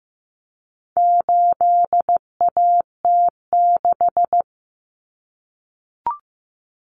Morse code, heard as a keyed tone, about 15 wpm, 700 Hz. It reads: 8AT6
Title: Common Cut Numbers - 4-Digit 15wpm